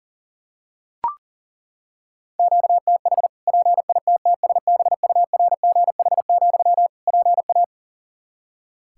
Morse code, heard as a keyed tone, about 37 wpm, 700 Hz. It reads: QTH PITTSBURGH, PA
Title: Practice QSO - Exchange 1 - Beginning - 40wpm